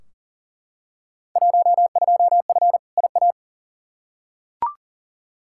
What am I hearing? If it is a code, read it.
Morse code, heard as a keyed tone, about 40 wpm, 700 Hz. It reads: 12F IA
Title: Field Day Exchanges Encore - 40wpm